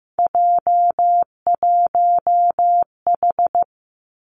Morse code, heard as a keyed tone, about 15 wpm, 700 Hz. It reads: J1H
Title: Three Challenging Letter-Number Combinations 15wpm